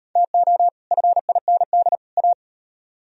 Morse code, heard as a keyed tone, about 38 wpm, 700 Hz. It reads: TO FIND A